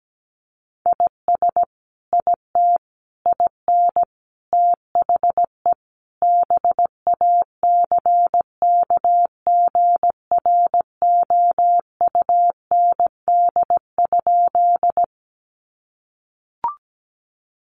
Morse code, heard as a keyed tone, about 17 wpm, 700 Hz. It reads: IS IT IN THE BACKGROUND?
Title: Sentences from Top 1300 Words 17wpm